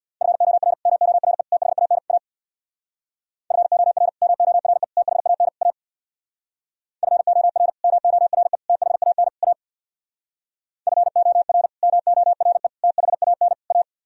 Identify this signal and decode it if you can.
Morse code, heard as a keyed tone, about 62 wpm, 700 Hz. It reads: FOR MORE THAN A FOR MORE THAN A FOR MORE THAN A FOR MORE THAN A
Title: Speed-Racing - Sets of 4 Words 50wpm